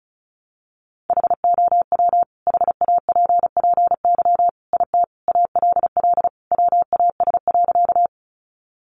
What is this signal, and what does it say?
Morse code, heard as a keyed tone, about 34 wpm, 700 Hz. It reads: HOW HAPPY IT ALL WAS.
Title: Sentences from Top 1000 Words 35wpm